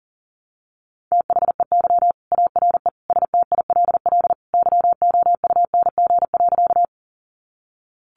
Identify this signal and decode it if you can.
Morse code, heard as a keyed tone, about 40 wpm, 700 Hz. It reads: THEY ARE STILL YOUNG.
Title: Sentences from Top 400 Words 40wpm